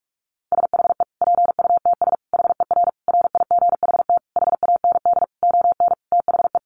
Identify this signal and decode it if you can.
Morse code, heard as a keyed tone, about 44 wpm, 700 Hz. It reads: SHE PUTS HER RIGHT HAND ON THE
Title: Easy Phrases 45wpm